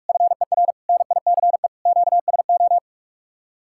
Morse code, heard as a keyed tone, about 49 wpm, 700 Hz. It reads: FER NICE QSO